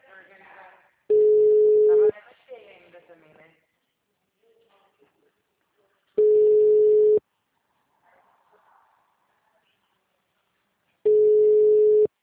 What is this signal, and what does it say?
Ringback tone (Europe, 425 Hz), with speech.
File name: out- -2029- - - .71981